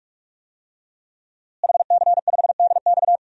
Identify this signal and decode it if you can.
Morse code, heard as a keyed tone, about 48 wpm, 700 Hz. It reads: HC5DX
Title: DXCC-100 Call Signs - 5 Characters - Repeated 3x - 45wpm